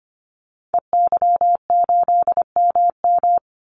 Morse code, heard as a keyed tone, about 25 wpm, 700 Hz. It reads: EY8MM
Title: Call Signs Repeated 2x - 25wpm